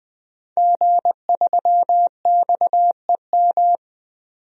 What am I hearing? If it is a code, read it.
Morse code, heard as a keyed tone, about 20 wpm, 700 Hz. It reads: G3XEM
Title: DXCC-100 Call Signs - 5 Characters - 20wpm